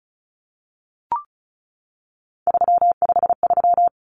Morse code, heard as a keyed tone, about 34 wpm, 700 Hz. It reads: 353
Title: Common Cut Numbers - 3-Digit 35wpm